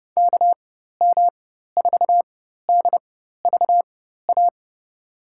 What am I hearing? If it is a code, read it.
Morse code, heard as a keyed tone, about 30 wpm, spaced as for 14 wpm, 700 Hz. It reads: KM4DVA
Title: USA Call Signs - 4x Character Spacing - 30wpm